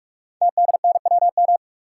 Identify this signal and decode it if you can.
Morse code, heard as a keyed tone, about 44 wpm, 700 Hz. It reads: TDNWM
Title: Code Groups - 5 Letters - 45wpm